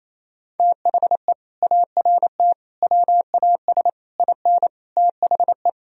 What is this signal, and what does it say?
Morse code, heard as a keyed tone, about 28 wpm, 700 Hz. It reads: THE ART WAS IN THE
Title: Sentences from Top 1000 Words - Review Part 3 - 28wpm